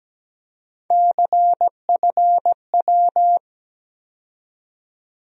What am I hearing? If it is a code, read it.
Morse code, heard as a keyed tone, about 17 wpm, 700 Hz. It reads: CFW
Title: USA Call Sign Postfixes - 3 Letters 17wpm